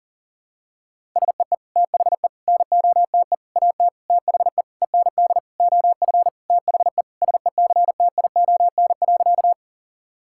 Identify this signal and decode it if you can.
Morse code, heard as a keyed tone, about 40 wpm, 700 Hz. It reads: SEE THE NOTE AT THE END OF THE SECTION.
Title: Sentences from Top 1000 Words - Review Part 5 - 40wpm